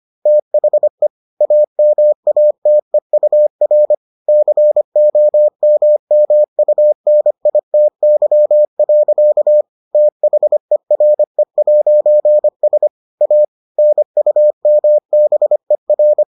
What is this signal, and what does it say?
Morse code, heard as a keyed tone, about 25 wpm, 600 Hz. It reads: THE AMATEUR COMMUNITY. THERE'S A NUMBER